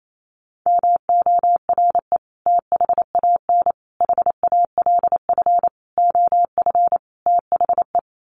Morse code, heard as a keyed tone, about 28 wpm, 700 Hz. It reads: MORE THAN HALF OF THE
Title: Sets of 5 Words 28wpm